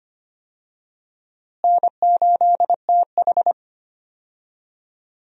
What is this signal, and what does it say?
Morse code, heard as a keyed tone, about 25 wpm, 700 Hz. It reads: N8TH
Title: USA Call Signs - 4 Characters - Repeated 3x - 25wpm